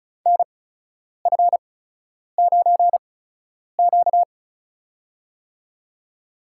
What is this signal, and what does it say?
Morse code, heard as a keyed tone, about 36 wpm, spaced as for 10 wpm, 700 Hz. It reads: NF9Q
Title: USA Call Signs - 8x Character Spacing - 35wpm